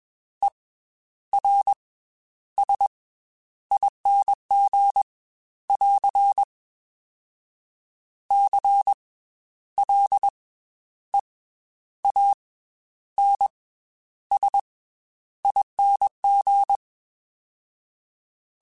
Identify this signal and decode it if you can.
Morse code, heard as a keyed tone, about 21 wpm, 800 Hz. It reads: E R S ING <AR> C L E A N S ING
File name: Special INGs / Special INGs-mp3s / Slow ing @20